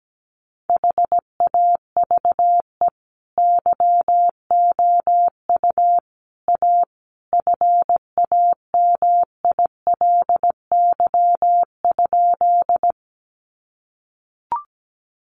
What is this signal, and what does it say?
Morse code, heard as a keyed tone, about 17 wpm, 700 Hz. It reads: HAVE YOU A FAMILY?
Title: Sentences from Top 400 Words 17wpm